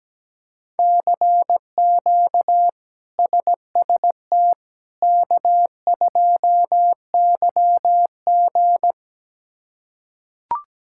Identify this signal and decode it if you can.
Morse code, heard as a keyed tone, about 17 wpm, 700 Hz. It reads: CQ SST K2YG
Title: K1USN SST QSO Practice - 17wpm